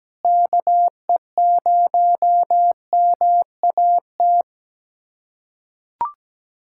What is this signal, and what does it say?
Morse code, heard as a keyed tone, about 17 wpm, 700 Hz. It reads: KE0MAT
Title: USA Callsign DE Callsign - 6 Characters - 17wpm